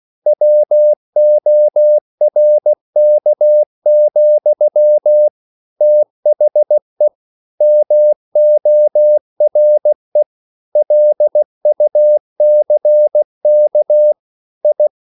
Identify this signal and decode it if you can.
Morse code, heard as a keyed tone, about 16 wpm, 600 Hz. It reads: WORK, THE MORE LUCK I
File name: CW / Practice / EnglishQuotations0116